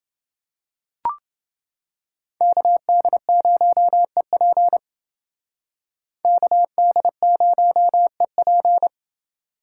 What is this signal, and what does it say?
Morse code, heard as a keyed tone, about 28 wpm, 700 Hz. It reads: KD0EP KD0EP